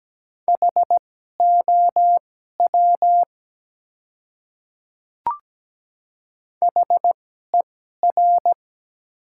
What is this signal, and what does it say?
Morse code, heard as a keyed tone, about 17 wpm, spaced as for 12 wpm, 700 Hz. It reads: HOW HER
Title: 2x Character Spacing - Top 200 Words - 17wpm